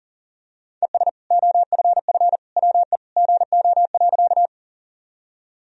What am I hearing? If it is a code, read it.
Morse code, heard as a keyed tone, about 40 wpm, 700 Hz. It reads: ES OFF WE GO.
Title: Sentences from Top 200 Words and Abbreviations 40wpm